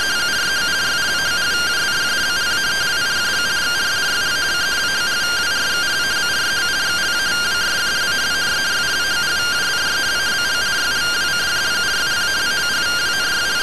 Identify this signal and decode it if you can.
Two-tone teleprinter signal, about 170 Hz shift.
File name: findings / MR AXIAL 33,99mm 1.gen-19 rtty rx